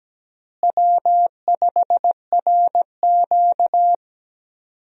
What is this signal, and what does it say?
Morse code, heard as a keyed tone, about 17 wpm, 700 Hz. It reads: W5RQ
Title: USA Call Signs - 4 Characters - Repeated 2x - 17wpm